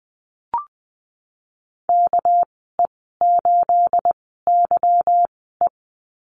Morse code, heard as a keyed tone, about 20 wpm, spaced as for 15 wpm, 700 Hz. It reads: KE8YE
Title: 213 Lesson 34g - Number 8 - Call Signs 20wpm